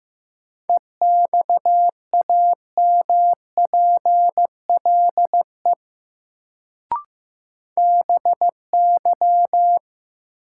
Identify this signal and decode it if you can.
Morse code, heard as a keyed tone, about 15 wpm, 700 Hz. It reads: EXAMPLE BY
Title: Sets of 2 Words - New Word Drill - 15wpm